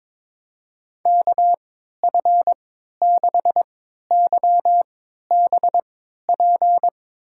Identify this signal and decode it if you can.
Morse code, heard as a keyed tone, about 21 wpm, spaced as for 12 wpm, 700 Hz. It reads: KF6YBP
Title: USA Call Signs - 3x Character Spacing - 22wpm